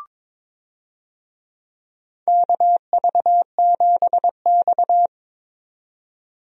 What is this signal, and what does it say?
Morse code, heard as a keyed tone, about 21 wpm, 700 Hz. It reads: KV7X
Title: USA Call Signs - 4 Characters - 22wpm